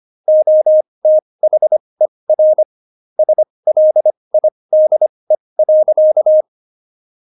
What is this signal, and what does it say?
Morse code, heard as a keyed tone, about 25 wpm, spaced as for 20 wpm, 620 Hz. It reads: OTHER SLIDE.
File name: Jokes-25-char-20-eff